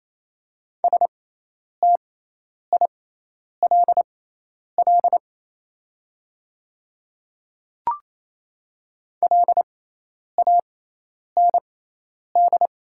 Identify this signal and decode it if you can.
Morse code, heard as a keyed tone, about 28 wpm, spaced as for 10 wpm, 700 Hz. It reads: STILL LAND
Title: 6x Character Spacing - Top 300 Words - 28wpm